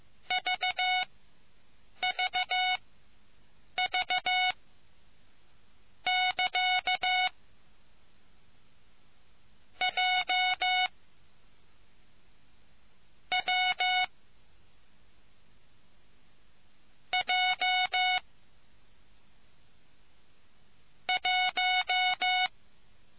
Morse code, heard as a keyed tone, about 15 wpm, spaced as for 6 wpm, 740 Hz. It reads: VVV<KA> J W J 1